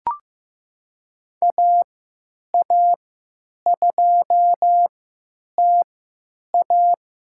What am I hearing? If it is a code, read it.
Morse code, heard as a keyed tone, about 15 wpm, spaced as for 9 wpm, 700 Hz. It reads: AA2TA
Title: USA Call Signs - 5 Characters - 3x Character Spacing - 15wpm